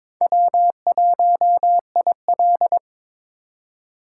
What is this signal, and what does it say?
Morse code, heard as a keyed tone, about 26 wpm, 700 Hz. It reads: W1IL
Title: Speed Racing - USA 1x2 Call Signs 22wpm